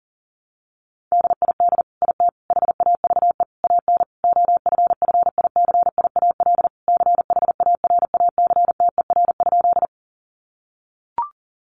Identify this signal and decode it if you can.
Morse code, heard as a keyed tone, about 40 wpm, 700 Hz. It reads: DID IT HAVE AN OFFICIAL CHARACTER?
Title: Sentences from Top 1200 Words 40wpm